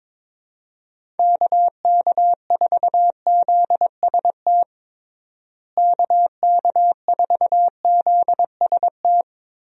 Morse code, heard as a keyed tone, about 22 wpm, 700 Hz. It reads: KK4ZST KK4ZST
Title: USA Call Signs - 6 Characters - Repeated 3x - 22wpm